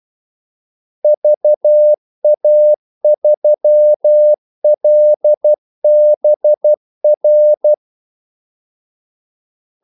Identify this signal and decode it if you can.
Morse code, heard as a keyed tone, about 12 wpm, 600 Hz. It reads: VA3LBR
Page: 2 ekstra ordmellemrum mellem kaldesignalerne